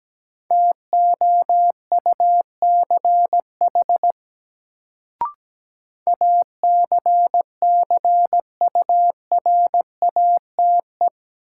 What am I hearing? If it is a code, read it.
Morse code, heard as a keyed tone, about 17 wpm, 700 Hz. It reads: TOUCH ACCURATE
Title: Sentences from Top 1200 Words - New Word Drill - 17wpm